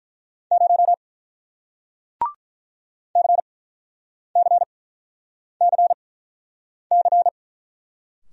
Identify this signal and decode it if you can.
Morse code, heard as a keyed tone, about 48 wpm, spaced as for 9 wpm, 700 Hz. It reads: 0 CCCC